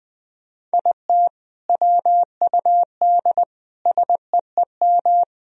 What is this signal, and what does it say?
Morse code, heard as a keyed tone, about 20 wpm, 700 Hz. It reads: IT WUD SEEM
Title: Sentences from Top 300 Words and Abbreviations 20wpm